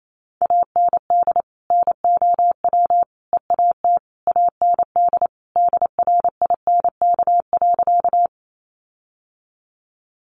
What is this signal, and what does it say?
Morse code, heard as a keyed tone, about 28 wpm, 700 Hz. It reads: AND NOW EAT AND DRINK.
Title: Sentences from Top 1500 Words 28wpm